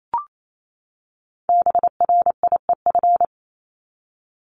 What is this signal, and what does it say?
Morse code, heard as a keyed tone, about 28 wpm, 700 Hz. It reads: BRIEF
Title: Top 1501-1750 Words 28wpm